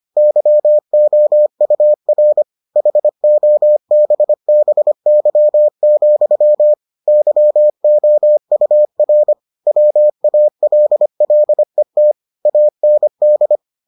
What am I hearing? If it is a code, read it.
Morse code, heard as a keyed tone, about 25 wpm, 600 Hz. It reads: YOUR HOBBY, YOUR WALLET AND